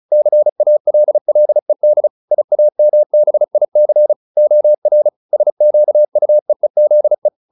Morse code, heard as a keyed tone, about 35 wpm, 600 Hz. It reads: CALLED IAMBIC OR SQUEEZE